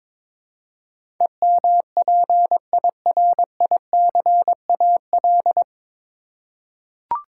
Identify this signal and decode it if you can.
Morse code, heard as a keyed tone, about 21 wpm, 700 Hz. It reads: EMPIRICAL
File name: practice / 22wpm / Nine Letter Words - 22wpm